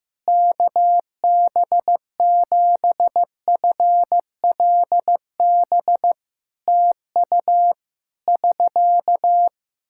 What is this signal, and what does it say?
Morse code, heard as a keyed tone, about 15 wpm, 700 Hz. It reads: KB7FLB TU <SK>
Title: Practice QSO - Exchange 4 - Ending - 15wpm